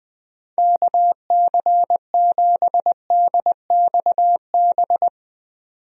Morse code, heard as a keyed tone, about 20 wpm, 700 Hz. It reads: KC7DXB
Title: USA Call Signs - 6 Characters - 20wpm